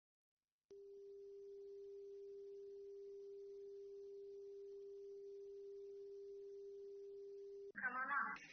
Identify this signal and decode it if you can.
Dial tone (Japan), 400 Hz, with speech.